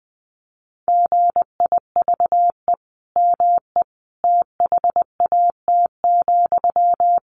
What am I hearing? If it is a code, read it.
Morse code, heard as a keyed tone, about 20 wpm, 700 Hz. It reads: GIVE ME THAT,
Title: Sentences from Top 1100 Words 20wpm